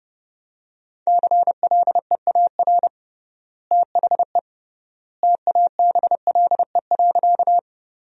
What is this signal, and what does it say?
Morse code, heard as a keyed tone, about 30 wpm, 700 Hz. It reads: CLEAR THE TABLE.
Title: 3x Word-Spacing - Sentences from Top 500 Words - 30wpm